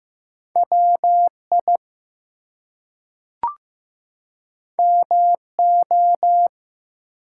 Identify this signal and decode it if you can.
Morse code, heard as a keyed tone, about 15 wpm, 700 Hz. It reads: WI MO
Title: US States and Canadian Provinces 15wpm